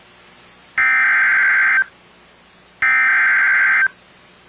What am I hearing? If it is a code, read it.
Emergency Alert System (SAME) tones (header bursts), heard as harsh buzzing bursts. It is heard from 0.8 to 3.8 s.